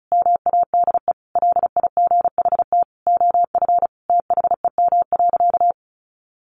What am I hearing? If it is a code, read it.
Morse code, heard as a keyed tone, about 34 wpm, 700 Hz. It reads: MADE LIGHT OF THEM.
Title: Sentences from Top 1300 Words 35wpm